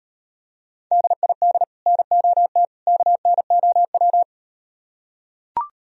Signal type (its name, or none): Morse code